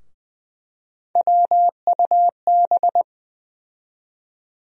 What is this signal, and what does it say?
Morse code, heard as a keyed tone, about 21 wpm, 700 Hz. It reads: WUB
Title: USA Call Sign Postfixes - 3 Letters 20wpm